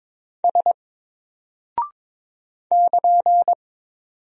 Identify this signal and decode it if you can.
Morse code, heard as a keyed tone, about 22 wpm, 700 Hz. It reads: S <KN>
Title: Single Character (Letter:Number:Punctuation:Prosign) Alt 22wpm